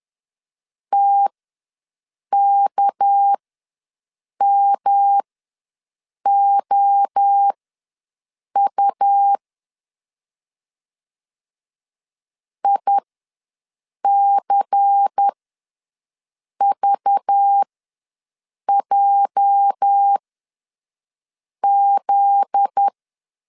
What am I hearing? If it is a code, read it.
Morse code, heard as a keyed tone, about 10 wpm, spaced as for 6 wpm, 800 Hz. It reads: TKMOU ICVJZ
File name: LEZIONI / 8 YJ45 / 8 YJ ws 10wpm 4043